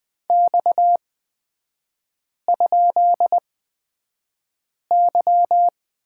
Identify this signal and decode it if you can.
Morse code, heard as a keyed tone, about 20 wpm, spaced as for 5 wpm, 690 Hz. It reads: X?Y
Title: 229 Lesson 37e - Letter x - Three Characters 20wpm